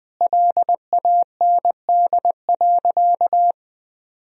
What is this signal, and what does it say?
Morse code, heard as a keyed tone, about 20 wpm, 700 Hz. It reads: LAND.
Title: Sentences from Top 1000 Words - Review Part 1 - 20wpm